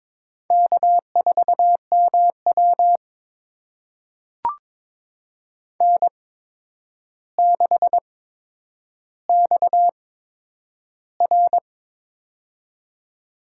Morse code, heard as a keyed tone, about 22 wpm, spaced as for 9 wpm, 700 Hz. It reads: K4MW N6XR
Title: USA Call Signs - 8x Character Spacing - 22wpm